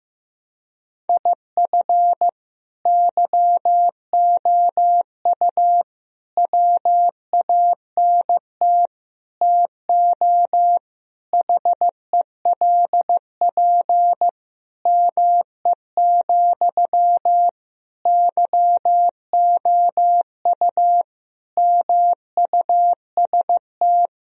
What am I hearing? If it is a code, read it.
Morse code, heard as a keyed tone, about 15 wpm, 700 Hz. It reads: IF YOU WANT TO HELP ME, YOU MUST
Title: Sentences from Top 1000 Words - Review Part 1 - 15wpm